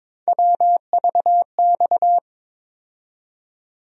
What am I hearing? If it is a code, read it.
Morse code, heard as a keyed tone, about 22 wpm, 700 Hz. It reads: WVX
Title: USA Call Sign Postfixes - 3 Letters 22wpm